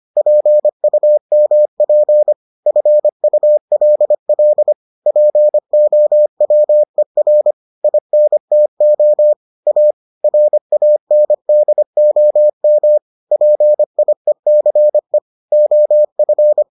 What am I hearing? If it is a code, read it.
Morse code, heard as a keyed tone, about 25 wpm, 600 Hz. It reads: PUMP FULL POWER INTO A RANDOM PIECE OF